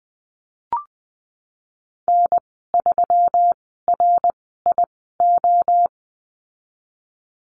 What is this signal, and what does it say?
Morse code, heard as a keyed tone, about 20 wpm, spaced as for 15 wpm, 700 Hz. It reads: N3RIO
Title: 122 Lesson 20g - Number 3 - Call Signs 20wpm